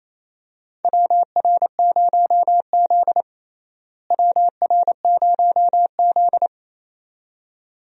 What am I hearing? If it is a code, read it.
Morse code, heard as a keyed tone, about 28 wpm, 700 Hz. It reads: WR0Z WR0Z
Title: USA Call Signs - 4 Characters - Repeated 2x - 28wpm